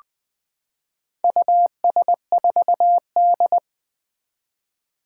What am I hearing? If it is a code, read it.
Morse code, heard as a keyed tone, about 20 wpm, 700 Hz. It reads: US4D
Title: Four Challenging Letter-Number Combinations 20wpm